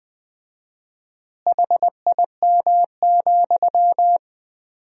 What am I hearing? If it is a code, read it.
Morse code, heard as a keyed tone, about 20 wpm, 700 Hz. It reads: HIM,
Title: 4x Word-Spacing - Sentences from Top 400 Words - 20wpm